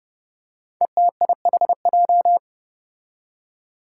Morse code, heard as a keyed tone, about 29 wpm, 700 Hz. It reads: ETIHJ
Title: Five Challenging Letter-Number Combinations 30wpm